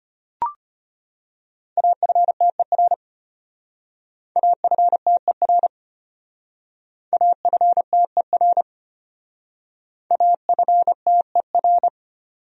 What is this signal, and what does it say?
Morse code, heard as a keyed tone, about 30 wpm, 700 Hz. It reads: AFTER AFTER AFTER AFTER
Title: Speed-Racing Top 500 Words 25wpm